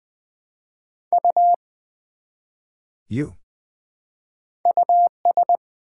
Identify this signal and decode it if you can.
Morse code, heard as a keyed tone, about 20 wpm, 700 Hz. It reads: U US
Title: Build A Word - 1 Letter At A Time - 3-Letter Words - 20wpm